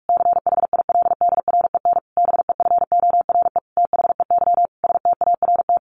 Morse code, heard as a keyed tone, about 44 wpm, 700 Hz. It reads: CHILDREN BEFORE THEY START
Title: Sentences from Top 1100 Words 45wpm